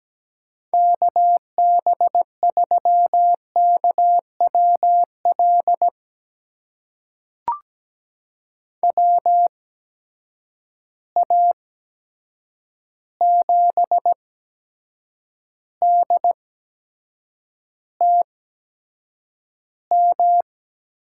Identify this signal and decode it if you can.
Morse code, heard as a keyed tone, about 17 wpm, spaced as for 7 wpm, 700 Hz. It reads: KB3KWL WA7DTM
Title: USA Call Signs - 6 Characters - 8x Character Spacing - 17wpm